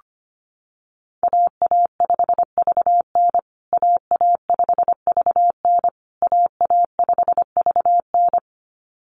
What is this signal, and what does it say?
Morse code, heard as a keyed tone, about 25 wpm, 700 Hz. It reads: AA5VN AA5VN AA5VN
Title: USA Call Signs Repeated 3x 25wpm